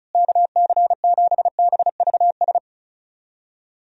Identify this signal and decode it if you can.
Morse code, heard as a keyed tone, about 34 wpm, 700 Hz. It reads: KC7BVS
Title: USA Call Signs - 6 Characters - Repeated 2x - 35wpm